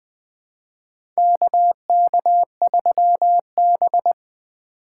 Morse code, heard as a keyed tone, about 20 wpm, 700 Hz. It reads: KK3B
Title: USA Call Signs - 4 Characters - 3x Character Spacing - 20wpm